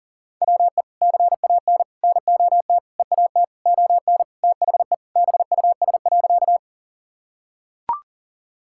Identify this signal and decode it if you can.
Morse code, heard as a keyed tone, about 38 wpm, 700 Hz. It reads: WE CAN NOT EAT ON THE BUS.
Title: Sentences from Top 1000 Words - Review Part 6 - 40wpm